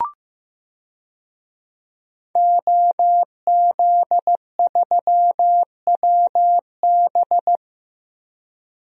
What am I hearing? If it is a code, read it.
Morse code, heard as a keyed tone, about 15 wpm, 700 Hz. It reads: OZ3WB
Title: DXCC-100 Call Signs - 5 Characters - Repeated 3x - 15wpm